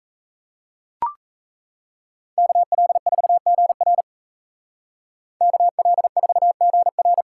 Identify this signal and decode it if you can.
Morse code, heard as a keyed tone, about 34 wpm, 700 Hz. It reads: KL4GR KL4GR
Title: Speed Racing - USA 2x2 Call Signs 28wpm